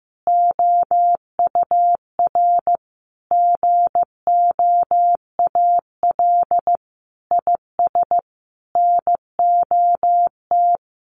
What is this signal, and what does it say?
Morse code, heard as a keyed tone, about 15 wpm, 700 Hz. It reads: OUR GOAL IS NOT
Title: Sentences from Top 1400 Words 15wpm